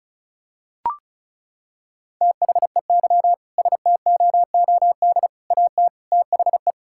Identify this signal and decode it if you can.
Morse code, heard as a keyed tone, about 34 wpm, 700 Hz. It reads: THEY STOOD AT THE
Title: Sentences from Top 1600 Words 35wpm